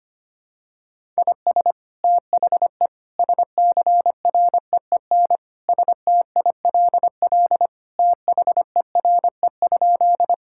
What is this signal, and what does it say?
Morse code, heard as a keyed tone, about 25 wpm, 700 Hz. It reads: IS THE SCREEN STILL THERE?
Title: Sentences from Top 1000 Words - Review Part 6 - 25wpm